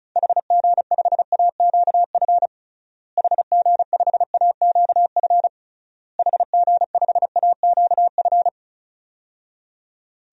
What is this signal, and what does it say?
Morse code, heard as a keyed tone, about 36 wpm, 700 Hz. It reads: HG5AQF HG5AQF HG5AQF
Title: DXCC-100 Call Signs - 6 Characters - Repeated 3x - 35wpm